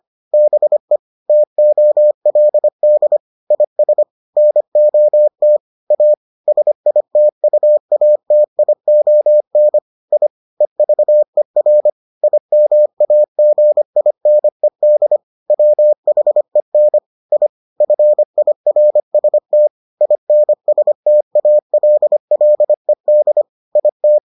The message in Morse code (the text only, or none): BE TOLD IS NOT A SITUATION I EVER IMAGINED WHEN I FIRST INSTALLED IT